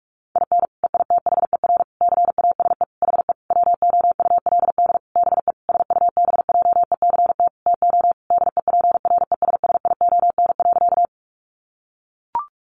Morse code, heard as a keyed tone, about 44 wpm, 700 Hz. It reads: IN EITHER CASE HE WOULD BE SUBJECT TO DEPRESSION.
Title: Sentences from Top 1300 Words 45wpm